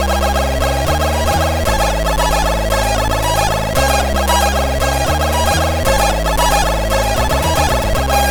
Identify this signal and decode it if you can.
There is no code.